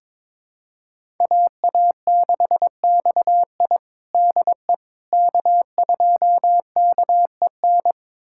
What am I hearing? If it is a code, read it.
Morse code, heard as a keyed tone, about 22 wpm, 700 Hz. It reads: AA6XI DE K2KEN
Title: USA Callsign DE Callsign - 5 Characters - 22wpm